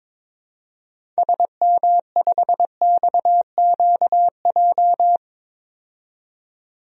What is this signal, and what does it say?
Morse code, heard as a keyed tone, about 22 wpm, 700 Hz. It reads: SM5XQJ
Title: DXCC-100 Call Signs - 6 Characters - 22wpm